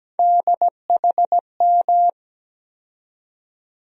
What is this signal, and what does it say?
Morse code, heard as a keyed tone, about 17 wpm, 700 Hz. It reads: DHM
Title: USA Call Sign Postfixes - 3 Letters 17wpm